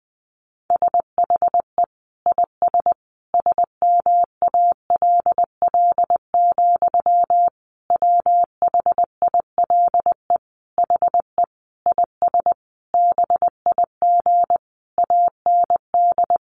Morse code, heard as a keyed tone, about 20 wpm, 700 Hz. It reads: SHE IS SMALL, WHILE HE IS BIG AND
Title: Sentences from Top 1300 Words 20wpm